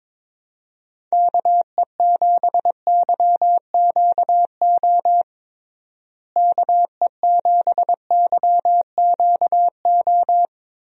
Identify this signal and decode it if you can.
Morse code, heard as a keyed tone, about 22 wpm, 700 Hz. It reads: KE7YQO KE7YQO
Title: USA Call Signs - 6 Characters - Repeated 2x - 22wpm